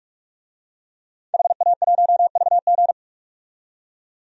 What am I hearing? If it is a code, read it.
Morse code, heard as a keyed tone, about 41 wpm, 700 Hz. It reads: HA1VG
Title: DXCC-100 Call Signs - 5 Characters - 45wpm